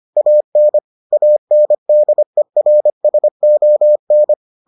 Morse code, heard as a keyed tone, about 25 wpm, 600 Hz. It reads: AN ANDERSON